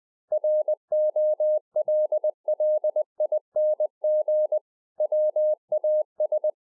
Morse code, heard as a keyed tone, about 20 wpm, 610 Hz. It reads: ROLLING WAS